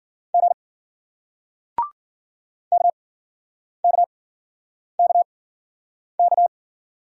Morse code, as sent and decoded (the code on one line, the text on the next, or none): --. / -.- -.- -.- -.-
G KKKK